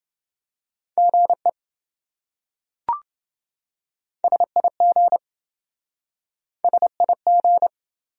Morse code, heard as a keyed tone, about 27 wpm, 700 Hz. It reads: GE SIG SIG